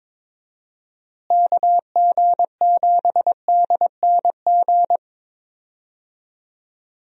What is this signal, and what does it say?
Morse code, heard as a keyed tone, about 21 wpm, 700 Hz. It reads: KG7DNG
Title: USA Call Signs - 6 Characters - Repeated 2x - 22wpm